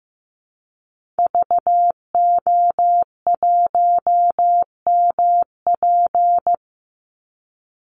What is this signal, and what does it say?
Morse code, heard as a keyed tone, about 15 wpm, 700 Hz. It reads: VO1MP
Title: Call Signs Repeated 2x - 15wpm